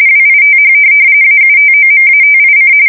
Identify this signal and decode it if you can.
RTTY, about 45.45 baud. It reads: WITH AUTOMATIC
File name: fsk170 / teletype-corp